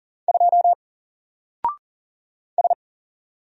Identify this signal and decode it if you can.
Morse code, heard as a keyed tone, about 40 wpm, 700 Hz. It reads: 2 S
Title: Single Letter-Number 40wpm